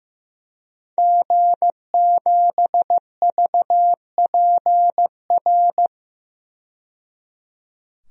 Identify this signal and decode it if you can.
Morse code, heard as a keyed tone, about 15 wpm, 700 Hz. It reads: G7VPR
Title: DXCC-100 Call Signs - 5 Characters - 15wpm